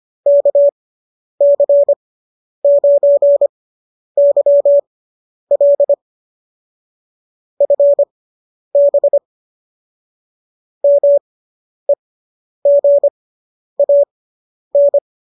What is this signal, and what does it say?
Morse code, heard as a keyed tone, about 25 wpm, spaced as for 10 wpm, 570 Hz. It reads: KC9YL FB MEGAN